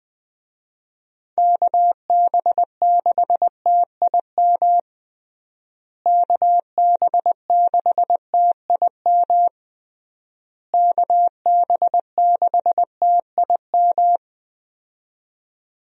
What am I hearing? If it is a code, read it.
Morse code, heard as a keyed tone, about 20 wpm, 700 Hz. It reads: KB6TIM KB6TIM KB6TIM
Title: USA Call Signs - 6 Characters - Repeated 3x - 20wpm